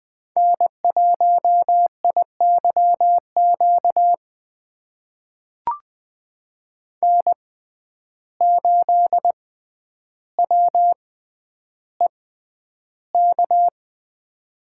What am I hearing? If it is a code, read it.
Morse code, heard as a keyed tone, about 20 wpm, spaced as for 10 wpm, 700 Hz. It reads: N1IYQ N8WEK
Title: USA Call Signs - 5 Characters - 6x Character Spacing - 20wpm